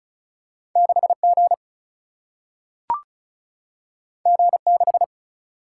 Morse code, heard as a keyed tone, about 36 wpm, 700 Hz. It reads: 6 G G 6
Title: Single-Letter Single-Number 35wpm